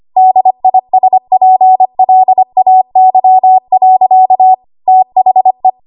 Morse code, heard as a keyed tone, about 25 wpm, 750 Hz. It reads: DISPLAY. THE
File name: Archive / 25 WPM / 25WPM